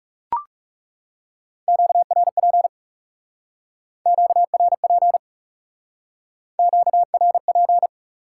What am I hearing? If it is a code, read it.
Morse code, heard as a keyed tone, about 39 wpm, 700 Hz. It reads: QRP QRP QRP